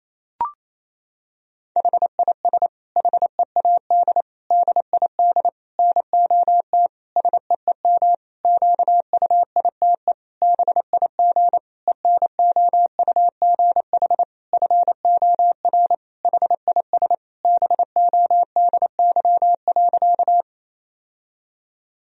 Morse code, heard as a keyed tone, about 28 wpm, 700 Hz. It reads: HIS HEAD DID NOT SEEM QUITE BIG ENOUGH FOR HIS BODY.
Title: Sentences from Top 1000 Words - Review Part 2 - 28wpm